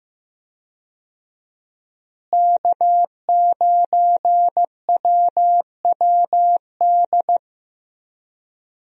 Morse code, heard as a keyed tone, about 15 wpm, 700 Hz. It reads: K9WWD
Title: USA Call Signs - 5 Characters - Repeated 3x - 15wpm